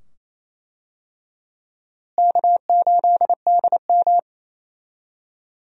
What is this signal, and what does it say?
Morse code, heard as a keyed tone, about 27 wpm, 700 Hz. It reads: K8DM
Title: USA Call Signs - 4 Characters - Repeated 2x - 28wpm